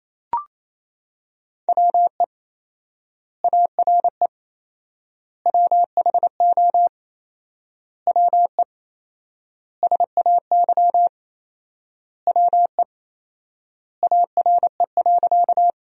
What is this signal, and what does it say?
Morse code, heard as a keyed tone, about 28 wpm, 700 Hz. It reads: WE ARE WHO WE SAY WE ARE.
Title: 4x Word-Spacing - Sentences from Top 200 Words - 28wpm